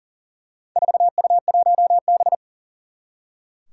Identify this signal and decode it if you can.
Morse code, heard as a keyed tone, about 40 wpm, 700 Hz. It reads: 4U1B
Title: Four Challenging Letter-Number Combinations 40wpm